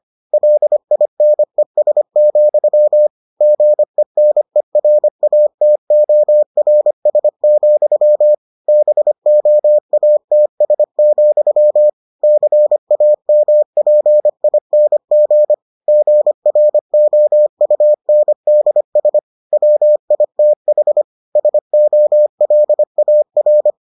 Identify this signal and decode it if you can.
Morse code, heard as a keyed tone, about 25 wpm, 600 Hz. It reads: LINES, GENERATORS, BOATS, CAMPING GROUNDS WITH SOLAR